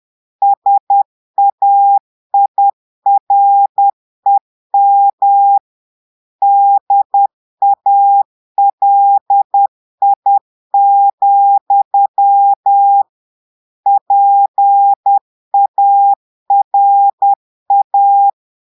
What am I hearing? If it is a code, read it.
Morse code, heard as a keyed tone, about 10 wpm, 800 Hz. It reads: SAIREM DALI, PARA
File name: portugais texte en clair 2 vitesse 10 mots minutes